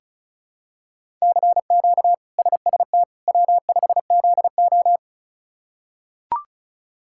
Morse code, heard as a keyed tone, about 34 wpm, 700 Hz. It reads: CQ SST W5ZO
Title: K1USN SST QSO Practice - 35wpm